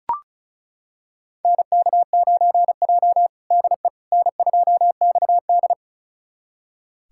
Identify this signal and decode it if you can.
Morse code, heard as a keyed tone, about 36 wpm, 700 Hz. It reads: NK9J DE N2XD
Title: USA Callsign DE Callsign - 4 Characters - 35wpm